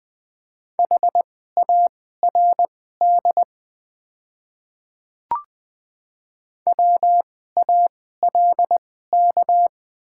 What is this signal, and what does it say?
Morse code, heard as a keyed tone, about 20 wpm, spaced as for 14 wpm, 690 Hz. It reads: HARD WALK
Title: 2x Character Spacing - Top 500 Words - 20wpm